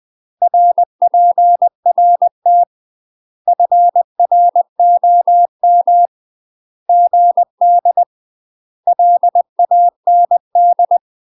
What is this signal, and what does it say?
Morse code, heard as a keyed tone, about 20 wpm, 700 Hz. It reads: RPRT FROM GD LAND
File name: UK MASTER QSOs TRAINING / 20WPM / UK MASTER QSOs 0034